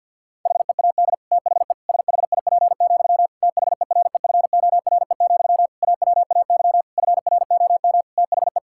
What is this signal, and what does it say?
Morse code, heard as a keyed tone, about 49 wpm, 700 Hz. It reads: HEAD THE SHIP, THEREFORE, AWAY FROM THE
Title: Sentences from Top 1000 Words - Review Part 5 - 50wpm